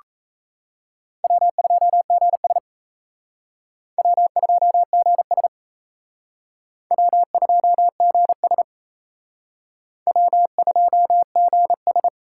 Morse code, heard as a keyed tone, about 35 wpm, 700 Hz. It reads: W2GS W2GS W2GS W2GS
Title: Speed Racing - USA Call Signs 28wpm